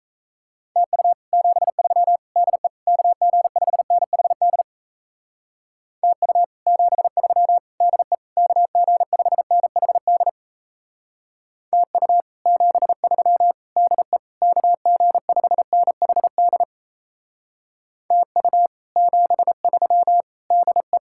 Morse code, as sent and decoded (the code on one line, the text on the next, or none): - ..- / --... ...-- / -.. / -.- --. ..... -. .... -.. / - ..- / --... ...-- / -.. / -.- --. ..... -. .... -.. / - ..- / --... ...-- / -.. / -.- --. ..... -. .... -.. / - ..- / --... ...-- / -.. .
TU 73 DE KG5NHD TU 73 DE KG5NHD TU 73 DE KG5NHD TU 73 DE